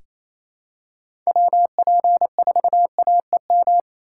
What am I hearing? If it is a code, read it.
Morse code, heard as a keyed tone, about 28 wpm, 700 Hz. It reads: WP4AEM
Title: USA Call Signs - 6 Characters - 2x Character Spacing - 28wpm